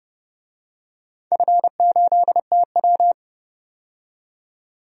Morse code, heard as a keyed tone, about 30 wpm, 700 Hz. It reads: F8TW
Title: DXCC-100 Call Signs - 4 Characters - 30wpm